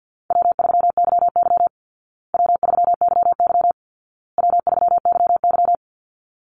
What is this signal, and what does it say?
Morse code, heard as a keyed tone, about 50 wpm, 700 Hz. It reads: W3YY W3YY W3YY
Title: Easy Call Signs Repeated 3x - 50wpm